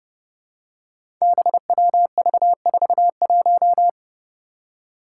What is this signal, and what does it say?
Morse code, heard as a keyed tone, about 30 wpm, 700 Hz. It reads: BWV41
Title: Five Challenging Letter-Number Combinations 30wpm